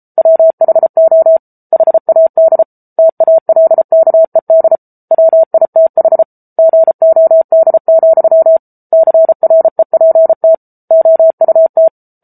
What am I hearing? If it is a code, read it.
Morse code, heard as a keyed tone, about 34 wpm, 650 Hz. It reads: WHO HAD TALKED WITH GOD, CREPT OUT
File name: Book Two / 07-Chapter 7 - 33 WPM